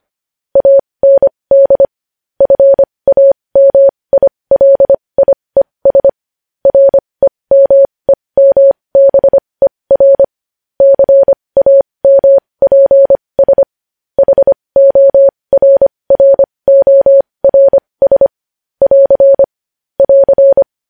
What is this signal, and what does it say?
Morse code, heard as a keyed tone, about 25 wpm, spaced as for 20 wpm, 550 Hz. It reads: AND FAMILIES REMEMBER CAMPS HORRORS <AR> <AR>
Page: news headlines in morse code at 20 wpm